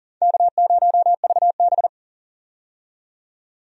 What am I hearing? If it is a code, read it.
Morse code, heard as a keyed tone, about 40 wpm, 700 Hz. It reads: K0VB
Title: USA Call Signs - 4 Characters - 40wpm